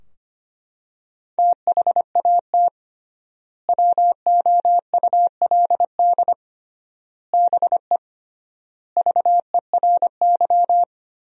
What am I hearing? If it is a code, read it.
Morse code, heard as a keyed tone, about 25 wpm, 700 Hz. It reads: THAT WOULD BE VERY
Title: 3x Word-Spacing - Sentences from Top 200 Words - 25wpm